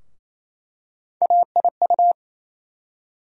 Morse code, heard as a keyed tone, about 28 wpm, 700 Hz. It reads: AIU
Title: USA Call Sign Postfixes - 3 Letters 28wpm